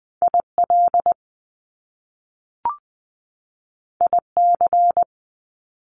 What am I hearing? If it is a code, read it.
Morse code, heard as a keyed tone, about 20 wpm, 700 Hz. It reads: IL IC
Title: Top Two Letter Combinations in Words 20wpm